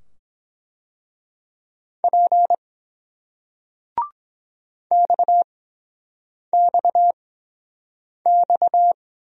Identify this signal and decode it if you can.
Morse code, heard as a keyed tone, about 22 wpm, spaced as for 7 wpm, 700 Hz. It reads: P XXX